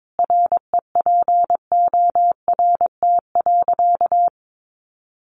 Morse code, heard as a keyed tone, about 22 wpm, 700 Hz. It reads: REPORT.
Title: Sentences from Top 1300 Words 22wpm